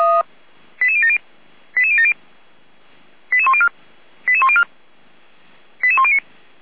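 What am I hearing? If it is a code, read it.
Five-tone selective call (ZVEI), heard as quick runs of five tones, tones of about 70 ms. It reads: 88070 88070 88104 88104 88109
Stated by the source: ZVEI five-tone selective call received on an SDR by the Feuerwehr Orlamünde fire brigade.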